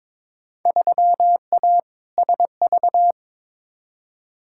Morse code, heard as a keyed tone, about 22 wpm, 700 Hz. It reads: 3A SV
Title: Field Day Exchanges - 22wpm